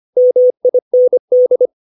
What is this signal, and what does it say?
Morse code, heard as a keyed tone, about 25 wpm, 500 Hz. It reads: MIND
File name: 632 words / 632 Words 25WPM